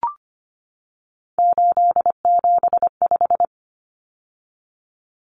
Morse code, heard as a keyed tone, about 25 wpm, 700 Hz. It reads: 875
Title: Numbers - 3-Digit 25wpm